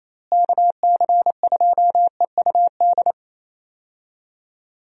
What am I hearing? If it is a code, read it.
Morse code, heard as a keyed tone, about 28 wpm, 700 Hz. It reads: KC2EUD
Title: USA Call Signs - 6 Characters - Repeated 2x - 28wpm